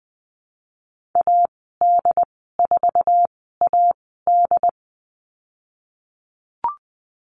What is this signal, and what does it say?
Morse code, heard as a keyed tone, about 19 wpm, spaced as for 14 wpm, 700 Hz. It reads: AD4AD
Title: 088 Lesson 15g - Number 5 - Call Signs 20wpm